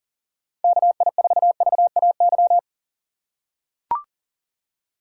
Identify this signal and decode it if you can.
Morse code, heard as a keyed tone, about 40 wpm, 700 Hz. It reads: KI4VAY
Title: USA Call Signs - 6 Characters - 2x Character Spacing - 40wpm